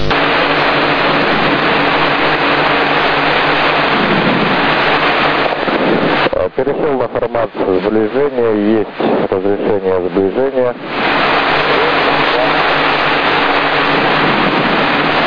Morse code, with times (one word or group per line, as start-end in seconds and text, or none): none